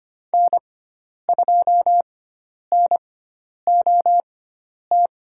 Morse code, heard as a keyed tone, about 25 wpm, spaced as for 10 wpm, 700 Hz. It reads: N2NOT
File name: Lesson 16g - Letter c - Call Signs 25wpm